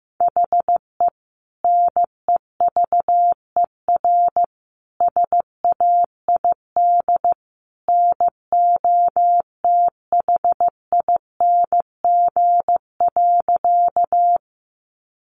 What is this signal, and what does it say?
Morse code, heard as a keyed tone, about 15 wpm, 700 Hz. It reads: HE NEVER SAID NOTHING.
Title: Sentences from Top 500 Words 15wpm